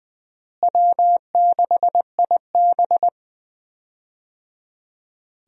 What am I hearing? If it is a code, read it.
Morse code, heard as a keyed tone, about 21 wpm, 700 Hz. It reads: W6IB
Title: USA Call Signs - 4 Characters - Repeated 3x - 20wpm